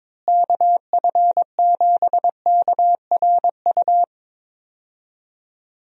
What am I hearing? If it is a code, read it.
Morse code, heard as a keyed tone, about 22 wpm, 700 Hz. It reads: KF7KRU
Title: USA Call Signs - 6 Characters - Repeated 3x - 22wpm